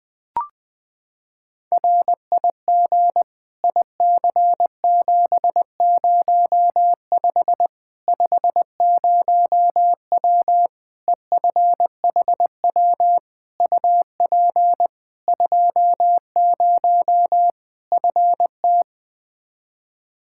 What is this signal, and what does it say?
Morse code, heard as a keyed tone, about 20 wpm, 700 Hz. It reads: RIG IC705 50W EFHW UP 20 FT
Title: Practice QSO - Exchange 2 - Info - 20wpm